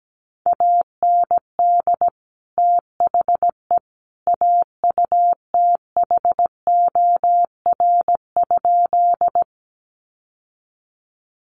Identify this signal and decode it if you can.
Morse code, heard as a keyed tone, about 17 wpm, 700 Hz. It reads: AND THE AUTHOR?
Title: Sentences from Top 1200 Words 17wpm